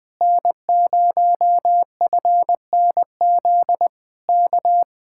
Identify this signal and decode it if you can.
Morse code, heard as a keyed tone, about 20 wpm, 700 Hz. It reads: N0FNZ K
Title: Speed-Racing Common QSO Phrases - 20wpm